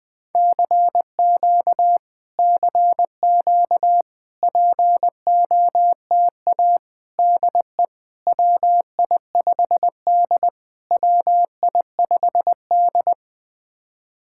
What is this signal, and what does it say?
Morse code, heard as a keyed tone, about 20 wpm, 700 Hz. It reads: CQ CQ POTA DE WI5D WI5D
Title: POTA QSO Practice - 20wpm